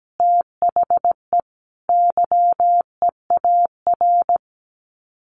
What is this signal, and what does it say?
Morse code, heard as a keyed tone, about 17 wpm, 700 Hz. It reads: THE YEAR
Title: Sets of 5 Words 17wpm